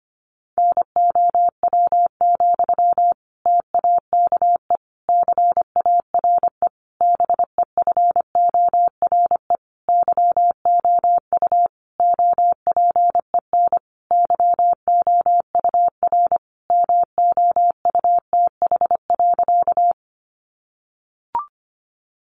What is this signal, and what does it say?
Morse code, heard as a keyed tone, about 25 wpm, 700 Hz. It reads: NOW, TAKE CARE BEFORE YOU OPEN YOUR MOUTH.
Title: Sentences from Top 1500 Words 25wpm